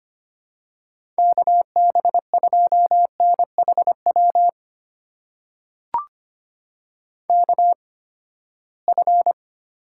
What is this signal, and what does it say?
Morse code, heard as a keyed tone, about 25 wpm, spaced as for 17 wpm, 700 Hz. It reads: KB2NHW KF
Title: USA Call Signs - 6 Characters - 8x Character Spacing - 25wpm